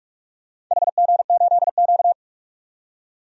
Morse code, heard as a keyed tone, about 43 wpm, 700 Hz. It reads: HG8Q
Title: DXCC-100 Call Signs - 4 Characters - 45wpm